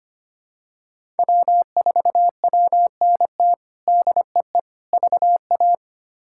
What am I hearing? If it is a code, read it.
Morse code, heard as a keyed tone, about 25 wpm, 700 Hz. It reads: W4WNT DEE VA
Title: K1USN SST QSO Practice - 25wpm